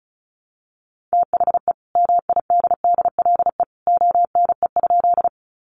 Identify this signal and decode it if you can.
Morse code, heard as a keyed tone, about 34 wpm, 700 Hz. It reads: THE MIDDLE ONE?
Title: Sentences from Top 900 Words 35wpm